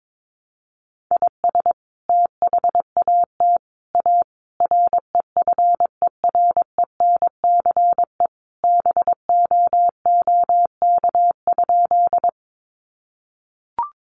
Morse code, heard as a keyed tone, about 22 wpm, 700 Hz. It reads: IS THAT A REFERENCE BOOK?
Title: Sentences from Top 1200 Words 22wpm